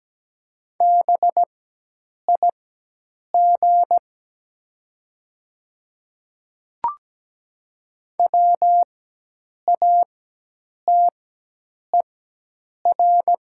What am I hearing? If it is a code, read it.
Morse code, heard as a keyed tone, about 17 wpm, spaced as for 8 wpm, 700 Hz. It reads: BIG WATER
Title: 4x Character Spacing - Top 300 Words - 17wpm